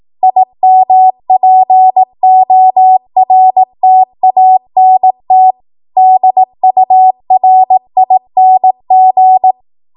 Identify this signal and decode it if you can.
Morse code, heard as a keyed tone, about 18 wpm, 750 Hz. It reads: IMPORTANT DURING
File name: Morse / 18WPM